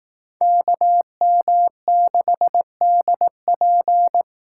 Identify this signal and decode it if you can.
Morse code, heard as a keyed tone, about 18 wpm, 700 Hz. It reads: KM6DP
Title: Speed Racing - USA 2x2 Call Signs 15wpm